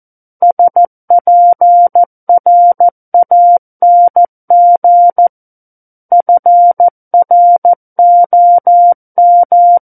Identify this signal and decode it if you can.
Morse code, heard as a keyed tone, about 14 wpm, 700 Hz. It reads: SPRANG FROM
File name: Book One / 05-Chapter 5 - 14 WPM